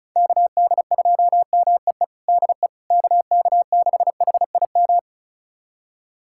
Morse code, heard as a keyed tone, about 34 wpm, 690 Hz. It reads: KD2MEE DE KK6HIM
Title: USA Callsign DE Callsign - 6 Characters - 35wpm